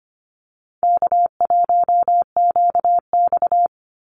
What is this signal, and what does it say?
Morse code, heard as a keyed tone, about 25 wpm, 700 Hz. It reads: K1QX
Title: Easy Call Signs Repeated 3x - 25wpm